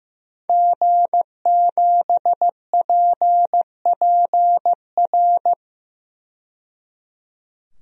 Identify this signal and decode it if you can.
Morse code, heard as a keyed tone, about 15 wpm, 700 Hz. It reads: G7PPR
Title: DXCC-100 Call Signs - 5 Characters - 15wpm